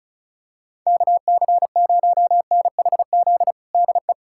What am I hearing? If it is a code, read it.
Morse code, heard as a keyed tone, about 34 wpm, 690 Hz. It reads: KC0NHZ DE
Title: USA Callsign DE Callsign - 6 Characters - 35wpm